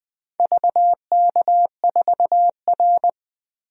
Morse code, heard as a keyed tone, about 20 wpm, 700 Hz. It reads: VK4R
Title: DXCC-100 Call Signs - 4 Characters - 20wpm